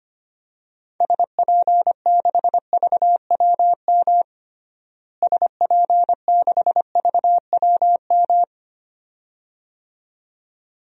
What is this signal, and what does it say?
Morse code, heard as a keyed tone, about 25 wpm, 700 Hz. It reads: SP6VWM SP6VWM
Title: DXCC-100 Call Signs - 6 Characters - Repeated 2x - 25wpm